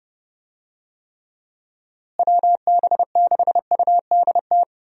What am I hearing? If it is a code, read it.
Morse code, heard as a keyed tone, about 30 wpm, 700 Hz. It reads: WB6UDT
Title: USA Call Signs - 6 Characters - Repeated 3x - 30wpm